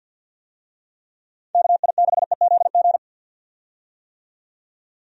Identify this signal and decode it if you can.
Morse code, heard as a keyed tone, about 45 wpm, 700 Hz. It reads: KI6EZG
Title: USA Call Signs - 6 Characters - 50wpm